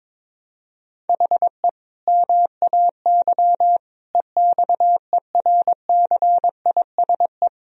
Morse code, heard as a keyed tone, about 22 wpm, 700 Hz. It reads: HE MAY EXERCISE
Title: Sentences from Top 1000 Words - Review Part 4 - 22wpm